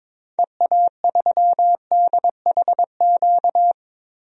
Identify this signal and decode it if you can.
Morse code, heard as a keyed tone, about 22 wpm, 700 Hz. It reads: EA3DHQ
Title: DXCC-100 Call Signs - 6 Characters - 22wpm